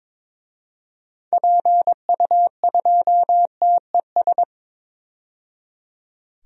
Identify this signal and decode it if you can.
Morse code, heard as a keyed tone, about 22 wpm, 700 Hz. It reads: PU2TES
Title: DXCC-100 Call Signs - 6 Characters - 22wpm